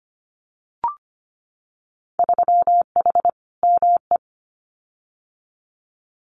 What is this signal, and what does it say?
Morse code, heard as a keyed tone, about 25 wpm, 700 Hz. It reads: 3H ME
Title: Winter Field Day Exchanges Encore 25wpm